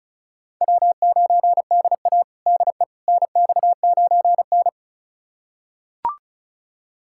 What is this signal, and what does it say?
Morse code, heard as a keyed tone, about 36 wpm, 700 Hz. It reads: W9DA DE NX9N
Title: USA Callsign DE Callsign - 4 Characters - 35wpm